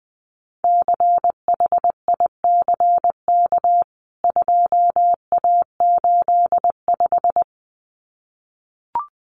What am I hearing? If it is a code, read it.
Morse code, heard as a keyed tone, about 20 wpm, 700 Hz. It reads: CHICK 2A85
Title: CWT Exchange 20wpm